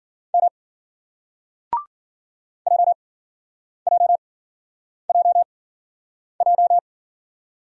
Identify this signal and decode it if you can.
Morse code, heard as a keyed tone, about 53 wpm, spaced as for 9 wpm, 700 Hz. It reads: M JJJJ